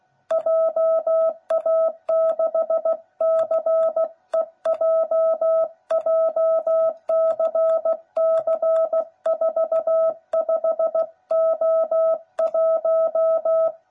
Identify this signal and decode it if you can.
Morse code, heard as a keyed tone, about 16 wpm, 660 Hz. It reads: JA6CEJ JCC45O1
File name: jh8cbh / 2021cw / cw09b